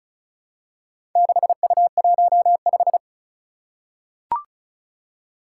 Morse code, heard as a keyed tone, about 36 wpm, 700 Hz. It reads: 6U15
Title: Four Challenging Letter-Number Combinations 35wpm